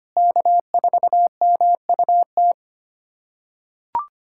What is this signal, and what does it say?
Morse code, heard as a keyed tone, about 25 wpm, 700 Hz. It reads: K4MUT
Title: USA Callsign DE Callsign - 5 Characters - 25wpm